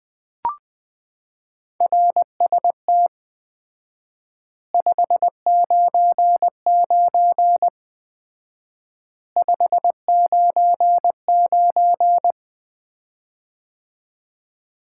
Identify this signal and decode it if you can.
Morse code, heard as a keyed tone, about 20 wpm, 700 Hz. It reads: RST 599 599
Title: Common QSO Phrases - 4x Word-Spacing - 20wpm